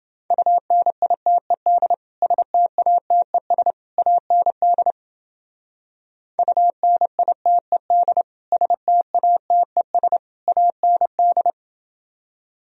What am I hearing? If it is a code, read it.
Morse code, heard as a keyed tone, about 30 wpm, 700 Hz. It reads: UNITED STATES AND UNITED STATES AND